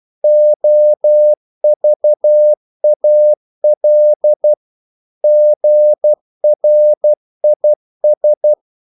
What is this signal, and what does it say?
Morse code, heard as a keyed tone, about 12 wpm, 600 Hz. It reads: OVAL GRIS
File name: Korte ord DK 12wpm